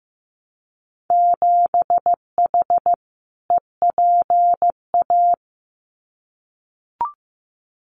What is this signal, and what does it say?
Morse code, heard as a keyed tone, about 15 wpm, 700 Hz. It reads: 7H EPA
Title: Winter Field Day Exchanges Encore 15wpm